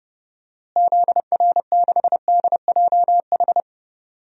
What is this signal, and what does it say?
Morse code, heard as a keyed tone, about 30 wpm, 700 Hz. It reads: ZR6DJH
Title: DXCC-100 Call Signs - 6 Characters - Repeated 2x - 30wpm